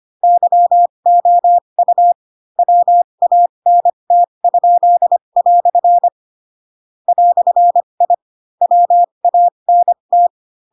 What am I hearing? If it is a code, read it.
Morse code, heard as a keyed tone, about 25 wpm, spaced as for 22 wpm, 700 Hz. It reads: YOU WANT?" "I WANT
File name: Pinocchio / Pinocchio ch2 18 25 ik1hge